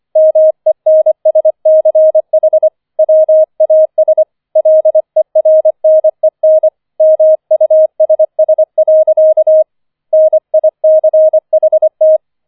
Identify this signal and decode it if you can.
Morse code, heard as a keyed tone, about 25 wpm, 630 Hz. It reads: MENSCH WAS LERNEN MUSS. NICHT